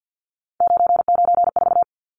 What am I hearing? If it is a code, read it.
Morse code, heard as a keyed tone, about 50 wpm, 700 Hz. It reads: 994
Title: Numbers - 3-Digit 50wpm